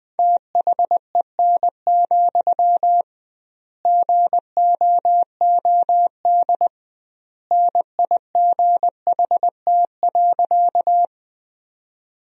Morse code, heard as a keyed tone, about 20 wpm, 700 Hz. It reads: THEN, GOOD NIGHT.
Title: 2x Word-Spacing - Sentences from Top 300 Words - 20wpm